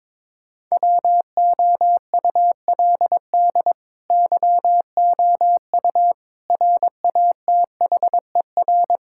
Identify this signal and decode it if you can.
Morse code, heard as a keyed tone, about 22 wpm, 700 Hz. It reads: WOULD YOU RATHER
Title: Sentences from Top 1000 Words - Review Part 6 - 22wpm